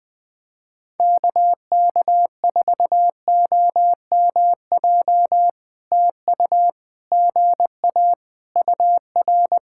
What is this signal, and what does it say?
Morse code, heard as a keyed tone, about 20 wpm, 700 Hz. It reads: KK4OMJ TU GA UR
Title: POTA QSO Practice - 20wpm